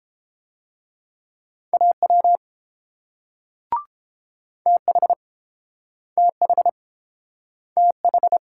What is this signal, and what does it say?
Morse code, heard as a keyed tone, about 32 wpm, 700 Hz. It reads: AW TH TH TH